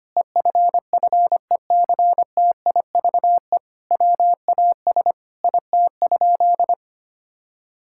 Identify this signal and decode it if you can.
Morse code, heard as a keyed tone, about 25 wpm, 700 Hz. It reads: EFFECTIVE WAS IT?
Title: Sentences from Top 1000 Words - Review Part 6 - 25wpm